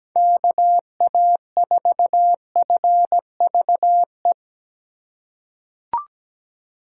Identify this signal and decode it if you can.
Morse code, heard as a keyed tone, about 17 wpm, 700 Hz. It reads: KA4FVE
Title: USA Call Signs - 6 Characters - 2x Character Spacing - 17wpm